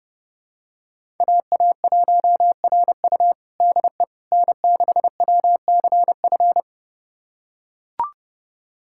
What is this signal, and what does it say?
Morse code, heard as a keyed tone, about 32 wpm, 700 Hz. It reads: AA1RU DE N6WCF
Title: USA Callsign DE Callsign - 5 Characters - 30wpm